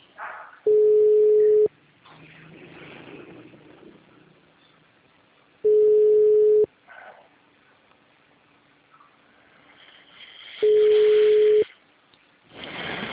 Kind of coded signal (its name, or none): telephone call progress tone